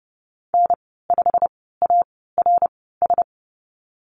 Morse code, heard as a keyed tone, about 30 wpm, spaced as for 17 wpm, 700 Hz. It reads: N5ARS
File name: course / 30:10wpm / 116 Lesson 19g - Letter w - Call Signs 30wpm